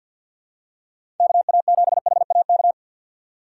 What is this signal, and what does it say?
Morse code, heard as a keyed tone, about 52 wpm, 700 Hz. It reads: KA7HAK
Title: USA Call Signs - 3x Character Spacing - 50wpm